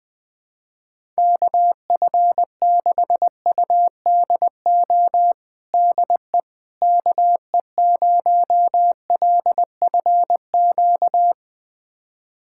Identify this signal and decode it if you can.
Morse code, heard as a keyed tone, about 20 wpm, 700 Hz. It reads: KF6UDO DE KE0LFQ
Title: USA Callsign DE Callsign 20wpm